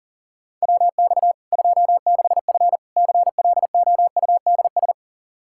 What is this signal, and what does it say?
Morse code, heard as a keyed tone, about 40 wpm, 700 Hz. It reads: WX 26F CLOUDS
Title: Practice QSO - Exchange 2 - Info - 40wpm